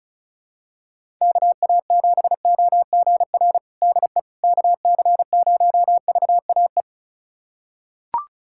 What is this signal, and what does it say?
Morse code, heard as a keyed tone, about 36 wpm, 700 Hz. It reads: KA7OGR DE KC0VAE
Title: USA Callsign DE Callsign - 6 Characters - 35wpm